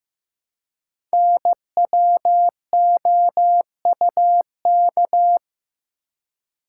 Morse code, heard as a keyed tone, about 15 wpm, 700 Hz. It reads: NWOUK
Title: Code Groups - 5 Letters - 15wpm